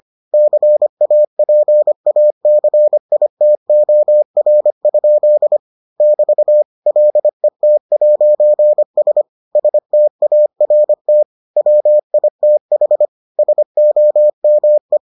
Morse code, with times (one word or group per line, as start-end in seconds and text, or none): 0.3-5.6 CAPACITOR?
6.0-6.6 <BT>
6.9-9.2 LET'S
9.6-11.2 START
11.6-13.1 WITH
13.4-15.0 SOME